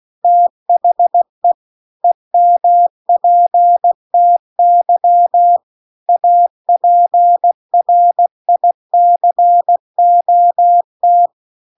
Morse code, heard as a keyed tone, about 16 wpm, 700 Hz. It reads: THE EMPTY APRICOT